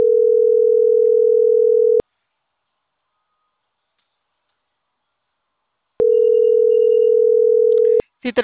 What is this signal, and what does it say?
Ringback tone (North America), 440 + 480 Hz.